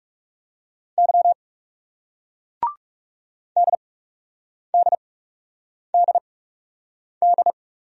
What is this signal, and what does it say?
Morse code, heard as a keyed tone, about 40 wpm, spaced as for 8 wpm, 700 Hz. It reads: Y DDDD